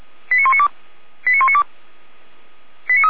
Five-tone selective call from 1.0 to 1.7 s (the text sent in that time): 89192